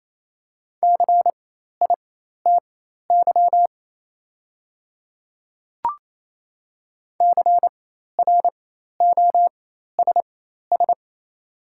Morse code, heard as a keyed tone, about 28 wpm, spaced as for 13 wpm, 700 Hz. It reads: CITY CROSS
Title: 4x Character Spacing - Top 300 Words - 28wpm